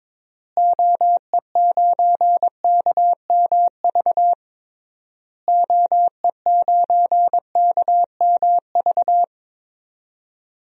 Morse code, heard as a keyed tone, about 22 wpm, 700 Hz. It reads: OE9KMV OE9KMV
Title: DXCC-100 Call Signs - 6 Characters - Repeated 3x - 22wpm